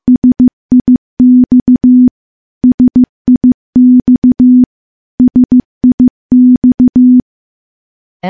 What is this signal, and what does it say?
Morse code, heard as a keyed tone, about 15 wpm, 250 Hz. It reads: SIX SIX SIX
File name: morse-training-15WPM